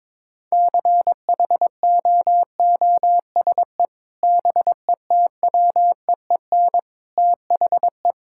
Morse code, heard as a keyed tone, about 21 wpm, 700 Hz. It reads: CHOOSE BETWEEN THE
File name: practice / 22wpm / Sentences with 3-Word Phrases - 22wpm